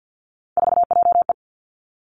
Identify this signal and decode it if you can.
Morse code, heard as a keyed tone, about 50 wpm, 700 Hz. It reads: 4JE
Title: Three Challenging Letter-Number Combinations 50wpm